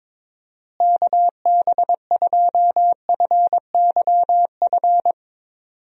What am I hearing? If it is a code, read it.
Morse code, heard as a keyed tone, about 22 wpm, 700 Hz. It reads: KB2FYF
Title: USA Call Signs - 6 Characters - Repeated 3x - 22wpm